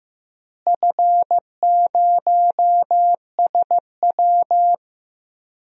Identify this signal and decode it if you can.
Morse code, heard as a keyed tone, about 15 wpm, 700 Hz. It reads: F0SW
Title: DXCC-100 Call Signs - 4 Characters - Repeated 3x - 15wpm